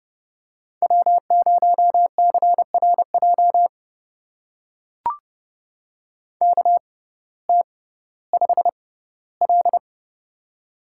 Morse code, heard as a keyed tone, about 30 wpm, spaced as for 17 wpm, 700 Hz. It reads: W0CRJ KT5L
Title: USA Call Signs - 6x Character Spacing - 30wpm